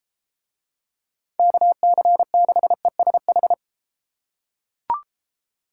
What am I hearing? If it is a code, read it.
Morse code, heard as a keyed tone, about 32 wpm, 700 Hz. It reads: KC6ESH